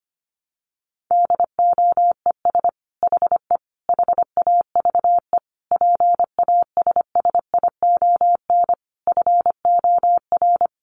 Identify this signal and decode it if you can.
Morse code, heard as a keyed tone, about 25 wpm, 700 Hz. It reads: DOES HE HAVE PASSION FOR
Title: Sentences from Top 1300 Words 25wpm